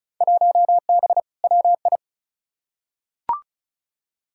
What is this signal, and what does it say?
Morse code, heard as a keyed tone, about 36 wpm, 700 Hz. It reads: 1B WI
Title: Field Day Exchanges - 35wpm